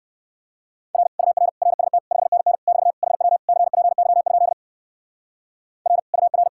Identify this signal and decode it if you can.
Morse code, heard as a keyed tone, about 52 wpm, 700 Hz. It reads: R UR RST 5NN <BT> HW CPY? R UR
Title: Speed-Racing Common QSO Phrases - 45wpm